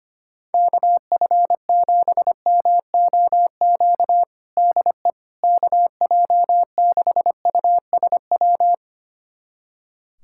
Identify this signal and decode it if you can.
Morse code, heard as a keyed tone, about 25 wpm, 700 Hz. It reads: KF7MOQ DE KJ6USW
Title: USA Callsign DE Callsign - 6 Characters - 25wpm